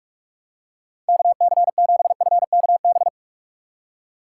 Morse code, heard as a keyed tone, about 46 wpm, 700 Hz. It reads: KC7FKB